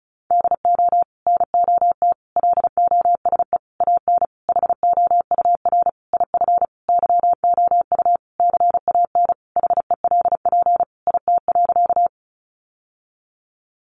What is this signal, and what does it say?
Morse code, heard as a keyed tone, about 34 wpm, 700 Hz. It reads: DO NOT LOSE AN HOUR IF YOU CAN HELP IT.
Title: Sentences from Top 1100 Words 35wpm